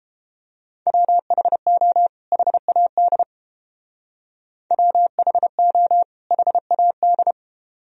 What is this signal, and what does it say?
Morse code, heard as a keyed tone, about 30 wpm, 700 Hz. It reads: WHO HAD WHO HAD